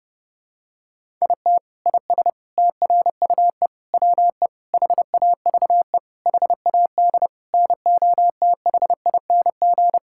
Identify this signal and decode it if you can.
Morse code, heard as a keyed tone, about 30 wpm, 700 Hz. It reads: IT IS TRUE WE HAVE HAD NOTHING
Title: Sentences from Top 1000 Words - Review Part 5 - 30wpm